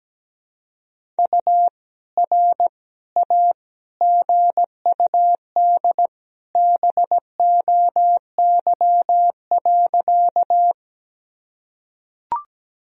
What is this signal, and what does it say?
Morse code, heard as a keyed tone, about 17 wpm, 700 Hz. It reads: U R A GUD BOY.
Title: Sentences from Top 200 Words and Abbreviations 17wpm